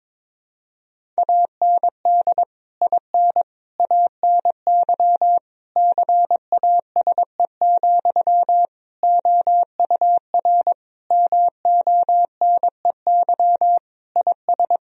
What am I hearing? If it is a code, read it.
Morse code, heard as a keyed tone, about 22 wpm, 700 Hz. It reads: AND IN ANY CASE, OUR MONEY IS
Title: Sentences from Top 1000 Words - Review Part 1 - 22wpm